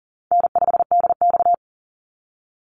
Morse code, heard as a keyed tone, about 42 wpm, 700 Hz. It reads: N5DX
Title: Easy Call Signs - 40wpm